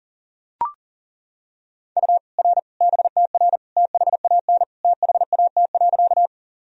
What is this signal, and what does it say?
Morse code, heard as a keyed tone, about 40 wpm, 700 Hz. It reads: U R BTR THAN THAT.
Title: Sentences from Top 400 Words and Abbreviations 40wpm